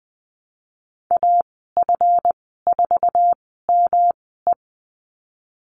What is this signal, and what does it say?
Morse code, heard as a keyed tone, about 20 wpm, spaced as for 15 wpm, 700 Hz. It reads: AF4ME
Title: 147 Lesson 24g - Letter y - Call Signs 20wpm